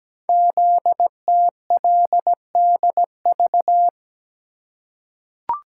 Morse code, heard as a keyed tone, about 17 wpm, 700 Hz. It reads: ZTLDV
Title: Code Groups - 5 Letters - 17wpm